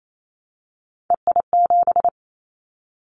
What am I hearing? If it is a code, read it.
Morse code, heard as a keyed tone, about 28 wpm, 700 Hz. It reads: EI7
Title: Callsign Prefixes for Top 50 DXCC - 28wpm